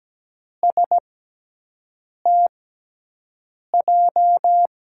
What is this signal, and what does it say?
Morse code, heard as a keyed tone, about 17 wpm, spaced as for 6 wpm, 700 Hz. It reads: STJ
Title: USA Call Signs - 6 Characters - 6x Character Spacing - 17wpm